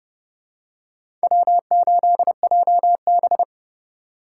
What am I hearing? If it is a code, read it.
Morse code, heard as a keyed tone, about 30 wpm, 700 Hz. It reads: W8JB
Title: USA Call Signs - 4 Characters - 3x Character Spacing - 30wpm